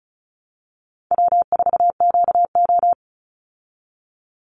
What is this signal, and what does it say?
Morse code, heard as a keyed tone, about 34 wpm, 700 Hz. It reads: W4QO
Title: Call Signs Repeated 3x - 35wpm